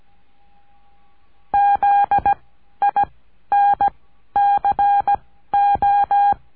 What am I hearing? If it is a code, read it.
Morse code, heard as a keyed tone, about 17 wpm, spaced as for 12 wpm, 810 Hz. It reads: ZINCO
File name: cursos / CURSO DE TELEGRAFIA / Telegrafia - Trilha 14